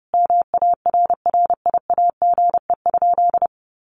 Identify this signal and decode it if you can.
Morse code, heard as a keyed tone, about 30 wpm, 700 Hz. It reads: MARRIAGE?
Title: Sentences from Top 1500 Words 30wpm